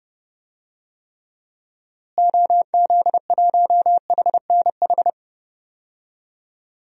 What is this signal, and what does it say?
Morse code, heard as a keyed tone, about 29 wpm, 700 Hz. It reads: OZ1HNH
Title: DXCC-100 Call Signs - 6 Characters - Repeated 3x - 30wpm